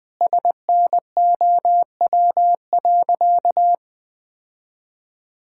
Morse code, heard as a keyed tone, about 20 wpm, 700 Hz. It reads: SNOW.
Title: Sentences with 8-Word Phrases - 20wpm